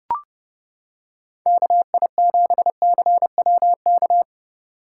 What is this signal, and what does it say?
Morse code, heard as a keyed tone, about 29 wpm, 700 Hz. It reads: KI7CWK